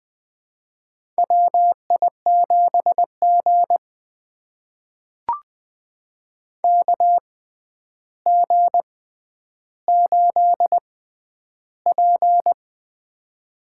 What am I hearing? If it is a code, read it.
Morse code, heard as a keyed tone, about 20 wpm, spaced as for 10 wpm, 700 Hz. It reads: WI7G KG8P
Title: USA Call Signs - 6x Character Spacing - 20wpm